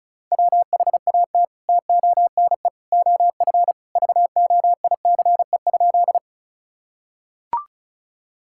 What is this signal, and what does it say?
Morse code, heard as a keyed tone, about 34 wpm, 700 Hz. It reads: WHAT TONE OF VOICE?
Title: Sentences from Top 1600 Words 35wpm